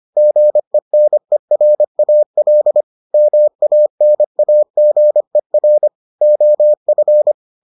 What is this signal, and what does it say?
Morse code, heard as a keyed tone, about 25 wpm, 600 Hz. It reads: GENERAL MANAGER OF